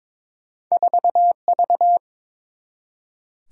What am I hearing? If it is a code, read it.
Morse code, heard as a keyed tone, about 22 wpm, 700 Hz. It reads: 4V
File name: practice / 22wpm / Two Letter-Number Combinations - 22wpm